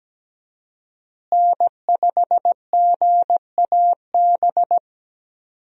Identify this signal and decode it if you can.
Morse code, heard as a keyed tone, about 17 wpm, 700 Hz. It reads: N5GAB
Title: USA Call Signs - 5 Characters - Repeated 2x - 17wpm